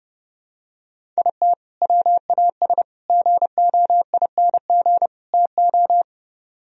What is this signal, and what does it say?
Morse code, heard as a keyed tone, about 28 wpm, 700 Hz. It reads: IT WAS GOING TO